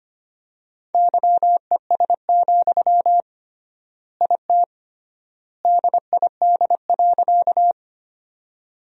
Morse code, heard as a keyed tone, about 25 wpm, 700 Hz. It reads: YES, IT DID.
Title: 3x Word-Spacing - Sentences from Top 500 Words - 25wpm